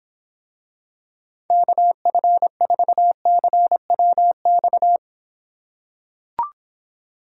Morse code, heard as a keyed tone, about 26 wpm, 700 Hz. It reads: KF4CWX